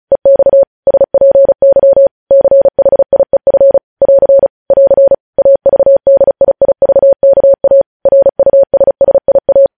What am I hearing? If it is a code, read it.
Morse code, heard as a keyed tone, about 36 wpm, 550 Hz. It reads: EX SPY CHIEF <AR> <AR> AVDIIVKA RUSSIA
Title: News Headlines at 35